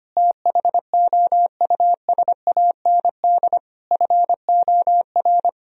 Morse code, heard as a keyed tone, about 25 wpm, 700 Hz. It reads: THOUSAND FOR
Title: Sentences from Top 1000 Words - Review Part 3 - 25wpm